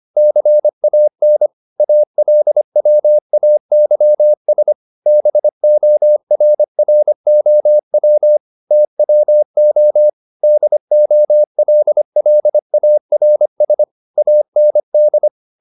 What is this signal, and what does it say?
Morse code, heard as a keyed tone, about 25 wpm, 600 Hz. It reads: CAN ALWAYS BORROW TWO DOLLARS AND
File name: the old man and the sea part 1 25wpm